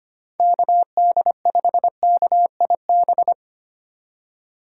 Morse code, heard as a keyed tone, about 25 wpm, 700 Hz. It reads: KD5KIB
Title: USA Call Signs - 6 Characters - 3x Character Spacing - 25wpm